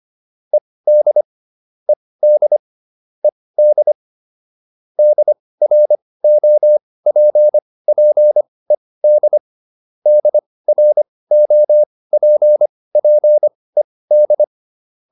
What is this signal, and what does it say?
Morse code, heard as a keyed tone, about 25 wpm, spaced as for 18 wpm, 600 Hz. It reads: ED ED ED DROPPED DROPPED